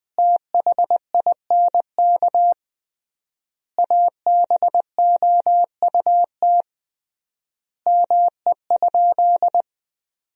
Morse code, heard as a keyed tone, about 20 wpm, 700 Hz. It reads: THINK ABOUT ME?
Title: 3x Word-Spacing - Sentences from Top 200 Words - 20wpm